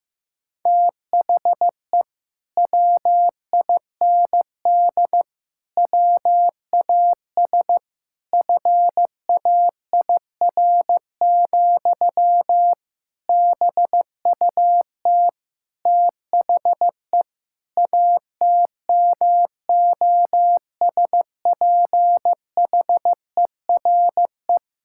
Morse code, heard as a keyed tone, about 15 wpm, 700 Hz. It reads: THE WIND WAS FAIR, BUT THE ATMOSPHERE
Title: Sentences from Top 1600 Words 15wpm